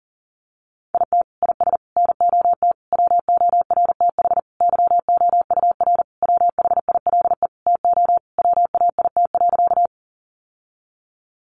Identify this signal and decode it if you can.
Morse code, heard as a keyed tone, about 40 wpm, 700 Hz. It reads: IT IS NOT WORTH YOUR WHILE TO WAIT.
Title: Sentences from Top 800 Words 40wpm